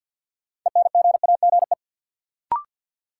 Morse code, heard as a keyed tone, about 50 wpm, 700 Hz. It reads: ENGAGE
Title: Sentences from Top 1500 Words - New Word Drill - 50wpm